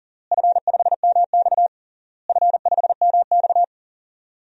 Morse code, heard as a keyed tone, about 40 wpm, 700 Hz. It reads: F5MX F5MX
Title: DXCC-100 Call Signs - 4 Characters - Repeated 3x - 40wpm